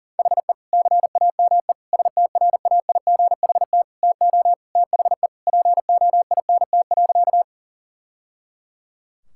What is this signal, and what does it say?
Morse code, heard as a keyed tone, about 37 wpm, 690 Hz. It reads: HE CAME STRAIGHT TO THE POINT.
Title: Sentences from Top 1000 Words - Review Part 6 - 40wpm